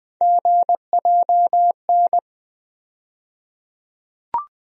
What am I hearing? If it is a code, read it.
Morse code, heard as a keyed tone, about 20 wpm, 700 Hz. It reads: GJN
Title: USA Call Sign Postfixes - 3 Letters 20wpm